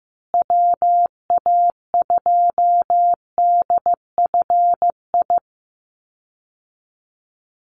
Morse code, heard as a keyed tone, about 15 wpm, 700 Hz. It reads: WA2DFI
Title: Call Signs Repeated 2x - 15wpm